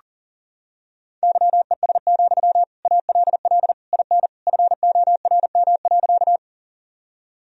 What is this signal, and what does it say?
Morse code, heard as a keyed tone, about 40 wpm, 700 Hz. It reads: YES, ALL IN FORM.
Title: Sentences from Top 1000 Words - Review Part 1 - 40wpm